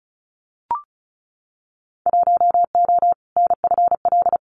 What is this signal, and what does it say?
Morse code, heard as a keyed tone, about 36 wpm, 700 Hz. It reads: 1O NFL
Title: Winter Field Day Exchanges 35wpm